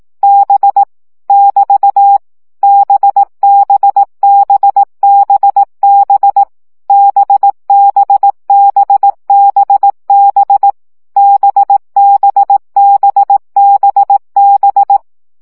Morse code, heard as a keyed tone, about 18 wpm, 800 Hz. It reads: B <BT> BBBBB BBBBB BBBBB
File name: CW gyakorlas b 15 baud